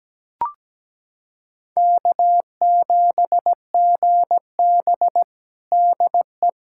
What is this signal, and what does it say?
Morse code, heard as a keyed tone, about 17 wpm, 700 Hz. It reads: K7GB DE
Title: USA Callsign DE Callsign - 4 Characters - 17wpm